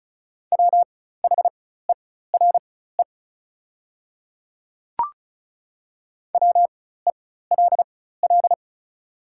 Morse code, heard as a keyed tone, about 36 wpm, spaced as for 16 wpm, 700 Hz. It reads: WHERE WELL
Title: 4x Character Spacing - Top 200 Words - 35wpm